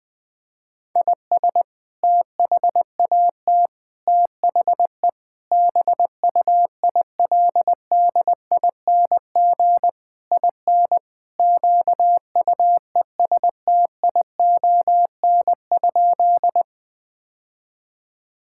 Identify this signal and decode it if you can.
Morse code, heard as a keyed tone, about 21 wpm, 700 Hz. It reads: IS THAT THE BUILDING IN QUESTION?
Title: Sentences from Top 1000 Words - Review Part 5 - 20wpm